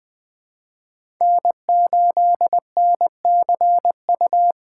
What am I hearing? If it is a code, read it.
Morse code, heard as a keyed tone, about 20 wpm, 700 Hz. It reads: N8NCU
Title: USA Call Signs - 5 Characters - 4x Character Spacing - 20wpm